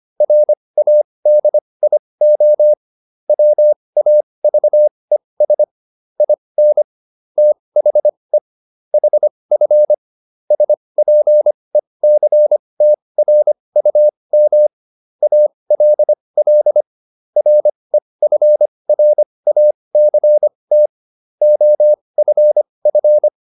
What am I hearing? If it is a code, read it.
Morse code, heard as a keyed tone, about 25 wpm, spaced as for 20 wpm, 600 Hz. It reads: RADIO WAVES IN THE HF SPECTRUM ALL REFRACT OFF